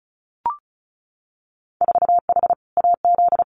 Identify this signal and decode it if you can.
Morse code, heard as a keyed tone, about 36 wpm, 700 Hz. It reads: 4H AZ
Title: Winter Field Day Exchanges Encore 35wpm